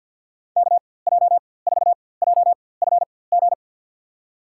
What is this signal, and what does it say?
Morse code, heard as a keyed tone, about 53 wpm, spaced as for 24 wpm, 700 Hz. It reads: KJ4JFG
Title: USA Call Signs - 6 Characters - 4x Character Spacing - 50wpm